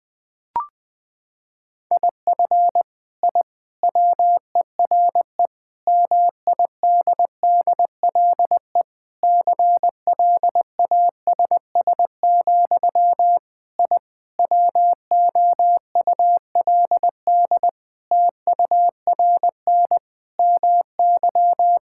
Morse code, heard as a keyed tone, about 20 wpm, 700 Hz. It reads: IF I WERE MIDDLE CLASS, I WOULD TURN MY
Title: Sentences from Top 1000 Words - Review Part 6 - 20wpm